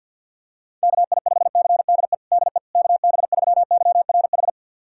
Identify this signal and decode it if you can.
Morse code, heard as a keyed tone, about 50 wpm, 700 Hz. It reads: KI5CDE DE KB3YRH
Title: USA Callsign DE Callsign 50wpm